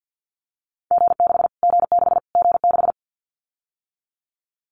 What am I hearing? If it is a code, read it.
Morse code, heard as a keyed tone, about 49 wpm, 700 Hz. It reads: G6 G6 G6
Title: Callsign Prefixes for Top 25 DXCC - Repeated 3x - 50wpm